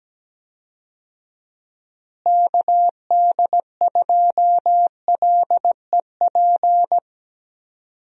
Morse code, heard as a keyed tone, about 17 wpm, 700 Hz. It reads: KD2LEP
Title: USA Call Signs - 6 Characters - Repeated 3x - 17wpm